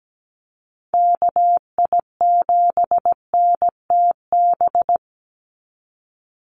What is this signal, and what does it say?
Morse code, heard as a keyed tone, about 17 wpm, 700 Hz. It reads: KI7NTB
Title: USA Call Signs 17wpm